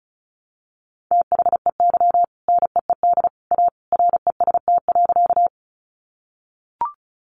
Morse code, heard as a keyed tone, about 34 wpm, 700 Hz. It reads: THEY NEED A REST.
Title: Sentences from Top 500 Words 35wpm